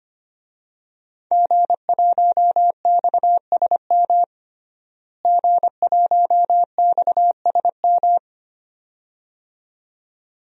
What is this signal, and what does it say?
Morse code, heard as a keyed tone, about 25 wpm, 700 Hz. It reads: G1XSM G1XSM
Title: DXCC-100 Call Signs - 5 Characters - Repeated 2x - 25wpm